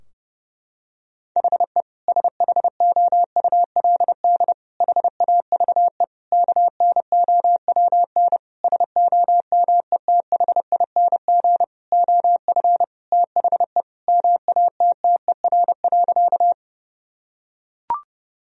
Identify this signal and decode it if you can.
Morse code, heard as a keyed tone, about 32 wpm, 700 Hz. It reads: HE SHOULD HAVE KNOWN SOMETHING OF THE MATTER.
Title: Sentences from Top 1000 Words - Review Part 3 - 30wpm